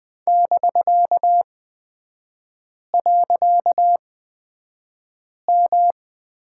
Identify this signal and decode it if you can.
Morse code, heard as a keyed tone, about 20 wpm, spaced as for 5 wpm, 700 Hz. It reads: <BK>.M
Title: 255 Lesson 41e - Period - Three Characters 20wpm